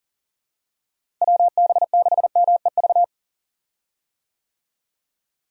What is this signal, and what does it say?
Morse code, heard as a keyed tone, about 37 wpm, 700 Hz. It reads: WB6MEV
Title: USA Call Signs - 6 Characters - Repeated 3x - 40wpm